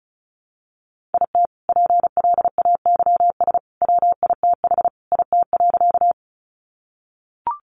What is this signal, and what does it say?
Morse code, heard as a keyed tone, about 34 wpm, 700 Hz. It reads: IT PLAYS WITH IT.
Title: Additional Easy Phrases 35wpm